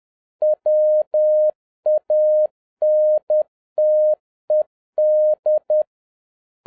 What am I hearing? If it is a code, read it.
Morse code, heard as a keyed tone, about 10 wpm, 610 Hz. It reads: WANTED
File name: 2011-07-mp3 / n36 THE 3.11 2 10wpm